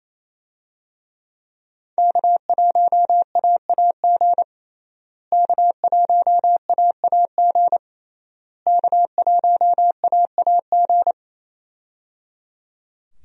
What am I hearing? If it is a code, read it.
Morse code, heard as a keyed tone, about 28 wpm, 700 Hz. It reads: K1AAG K1AAG K1AAG
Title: USA Call Signs - 5 Characters - Repeated 3x - 28wpm